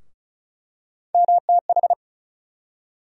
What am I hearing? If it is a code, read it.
Morse code, heard as a keyed tone, about 36 wpm, 700 Hz. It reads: MTH
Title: USA Call Sign Postfixes - 3 Letters 35wpm